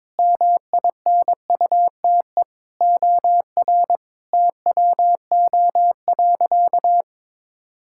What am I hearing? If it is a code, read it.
Morse code, heard as a keyed tone, about 22 wpm, 700 Hz. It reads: MINUTE OR TWO.
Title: Sentences from Top 1600 Words 22wpm